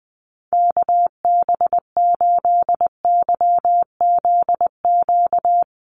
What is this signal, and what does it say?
Morse code, heard as a keyed tone, about 20 wpm, 700 Hz. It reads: KB8YZQ
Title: USA Call Signs 20wpm